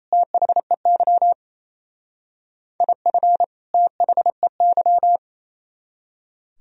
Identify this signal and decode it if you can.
Morse code, heard as a keyed tone, about 36 wpm, 700 Hz. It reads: THEY IF THEY